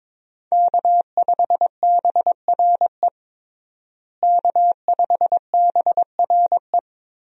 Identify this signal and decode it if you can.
Morse code, heard as a keyed tone, about 22 wpm, 700 Hz. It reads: K5BRE K5BRE
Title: USA Call Signs - 5 Characters - Repeated 3x - 22wpm